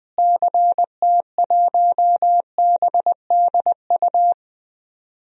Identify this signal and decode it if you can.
Morse code, heard as a keyed tone, about 20 wpm, 700 Hz. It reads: CT1BDU
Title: DXCC-100 Call Signs - 6 Characters - Repeated 2x - 20wpm